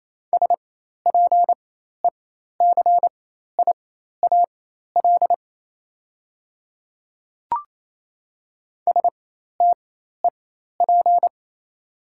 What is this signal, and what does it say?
Morse code, heard as a keyed tone, about 28 wpm, spaced as for 13 wpm, 690 Hz. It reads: SPECIAL STEP
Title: 4x Character Spacing - Top 500 Words - 28wpm